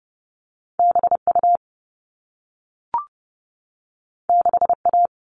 Morse code, heard as a keyed tone, about 30 wpm, 700 Hz. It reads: BU 6A
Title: All Cut Numbers - 2-Digit 30wpm